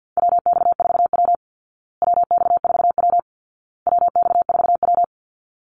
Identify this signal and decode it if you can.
Morse code, heard as a keyed tone, about 50 wpm, 700 Hz. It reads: WX4W WX4W WX4W
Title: Call Signs Repeated 2x - 50wpm